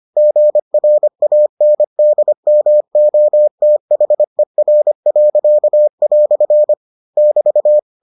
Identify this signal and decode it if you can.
Morse code, heard as a keyed tone, about 25 wpm, 600 Hz. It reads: GRANDMOTHER." <BT>